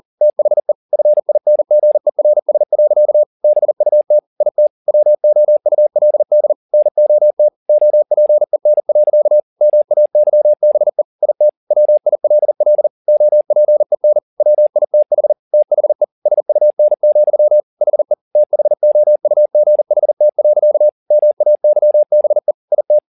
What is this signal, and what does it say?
Morse code, heard as a keyed tone, about 40 wpm, 600 Hz. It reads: THE FINGERS. BUT IT WOULD NOT OPEN. MAYBE IT WILL OPEN WITH THE SUN, HE THOUGHT. MAYBE IT